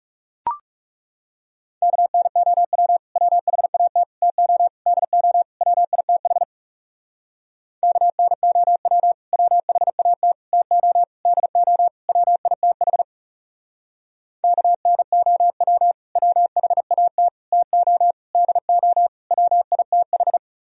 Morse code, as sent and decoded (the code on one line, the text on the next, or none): -.- -. --- .-- / .-- .... .- - / - --- / -.. --- / .-- .. - .... / -.- -. --- .-- / .-- .... .- - / - --- / -.. --- / .-- .. - .... / -.- -. --- .-- / .-- .... .- - / - --- / -.. --- / .-- .. - ....
KNOW WHAT TO DO WITH KNOW WHAT TO DO WITH KNOW WHAT TO DO WITH